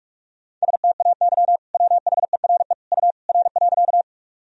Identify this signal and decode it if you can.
Morse code, heard as a keyed tone, about 46 wpm, 700 Hz. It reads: STAY WHERE U R.
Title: Sentences from Top 500 Words and Abbreviations 45wpm